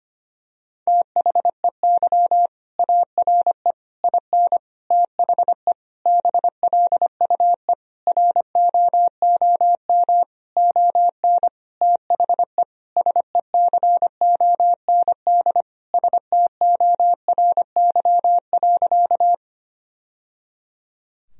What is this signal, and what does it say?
Morse code, heard as a keyed tone, about 25 wpm, 700 Hz. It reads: THEY ARE IN THE BLUE ROOM ON THE SECOND STORY.
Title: Sentences from Top 1000 Words - Review Part 3 - 25wpm